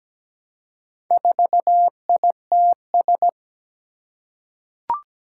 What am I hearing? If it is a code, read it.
Morse code, heard as a keyed tone, about 17 wpm, 700 Hz. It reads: 4ITS
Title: Four Challenging Letter-Number Combinations 17wpm